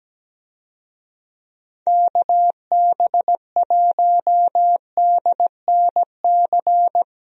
Morse code, heard as a keyed tone, about 17 wpm, 700 Hz. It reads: KB1DNC
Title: USA Call Signs - 6 Characters - 17wpm